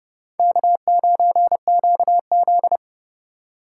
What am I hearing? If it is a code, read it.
Morse code, heard as a keyed tone, about 26 wpm, 700 Hz. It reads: K9QZ